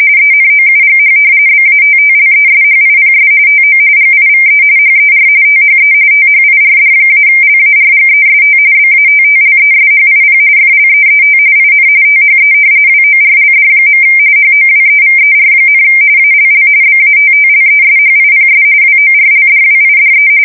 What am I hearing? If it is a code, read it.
RTTY, about 45.45 baud. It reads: ENOUGH BATTLESHIPS TO GIVE HER THE THIRD-LARGEST BATTLESHIP FLEET IN THE WORLD. HOWEVER, ALL BATTLESHIPS WORLDWIDE
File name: fsk170 / battleshipnj